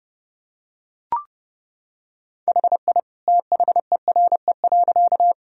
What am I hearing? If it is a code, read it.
Morse code, heard as a keyed tone, about 31 wpm, 700 Hz. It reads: HI THERE.
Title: Sentences from Top 1000 Words - Review Part 3 - 30wpm